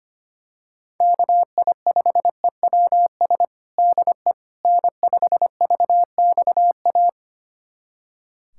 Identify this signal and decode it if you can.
Morse code, heard as a keyed tone, about 25 wpm, 700 Hz. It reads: KI5EWS DE N5VXA
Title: USA Callsign DE Callsign 25wpm